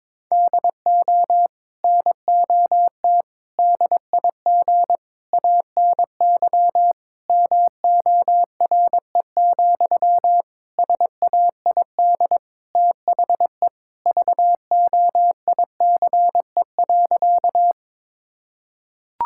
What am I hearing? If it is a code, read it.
Morse code, heard as a keyed tone, about 22 wpm, 700 Hz. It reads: DO NOT DIG ANY MORE, SAID THE VOICE.
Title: Sentences from Top 1600 Words 22wpm